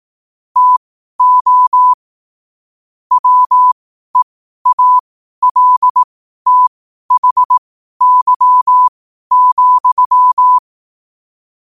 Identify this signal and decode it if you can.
Morse code, heard as a keyed tone, about 18 wpm, spaced as for 12 wpm, 1000 Hz. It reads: TO WEALTHY,
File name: qotdfiles / 12wpm